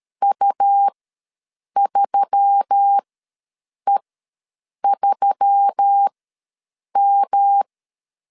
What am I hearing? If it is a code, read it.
Morse code, heard as a keyed tone, about 13 wpm, spaced as for 7 wpm, 800 Hz. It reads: U3E3M
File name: 5 LB ws 12wpm